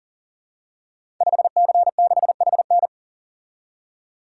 Morse code, heard as a keyed tone, about 44 wpm, 690 Hz. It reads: 5C6HN
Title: Code Groups - 5 Letter:Numbers - 40wpm